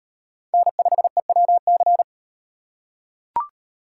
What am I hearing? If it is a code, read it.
Morse code, heard as a keyed tone, about 36 wpm, 700 Hz. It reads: N5EWC